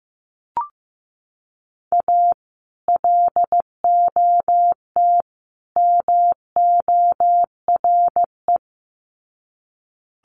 Morse code, heard as a keyed tone, about 15 wpm, 700 Hz. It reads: A LOT MORE
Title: Sets of 3 Words 15wpm